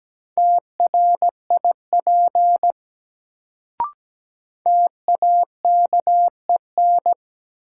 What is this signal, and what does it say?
Morse code, heard as a keyed tone, about 17 wpm, 700 Hz. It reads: TRIP TAKEN
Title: Sentences from Top 1000 Words - Review Part 4 - New Word Drill - 17wpm